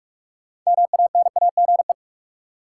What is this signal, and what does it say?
Morse code, heard as a keyed tone, about 46 wpm, 700 Hz. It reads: MANAGE
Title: Six Letter Words - 45wpm